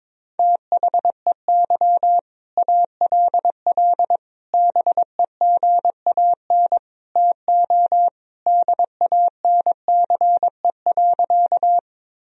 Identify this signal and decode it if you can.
Morse code, heard as a keyed tone, about 22 wpm, 700 Hz. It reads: THEY ALL BEGAN TO DANCE.
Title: Sentences from Top 1000 Words - Review Part 4 - 22wpm